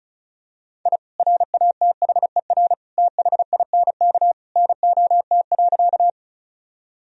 Morse code, heard as a keyed tone, about 34 wpm, 700 Hz. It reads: I RATHER THINK NOT.
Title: Sentences from Top 1000 Words - Review Part 4 - 35wpm